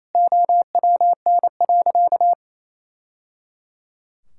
Morse code, heard as a keyed tone, about 28 wpm, 700 Hz. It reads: OWN.
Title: Sentences from Top 1000 Words - Review Part 6 - 28wpm